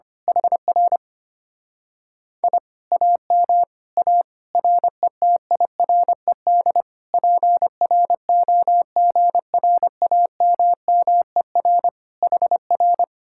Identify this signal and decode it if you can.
Morse code, heard as a keyed tone, about 24 wpm, 700 Hz. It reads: HR I AM A RETIRED PROGRAMMER HR
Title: Speed-Racing Common QSO Phrases Encore - 25wpm